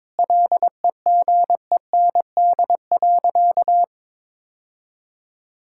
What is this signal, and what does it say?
Morse code, heard as a keyed tone, about 22 wpm, 700 Hz. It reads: LEGEND.
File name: practice / 22wpm / Sentences with 2-Word Phrases - 22wpm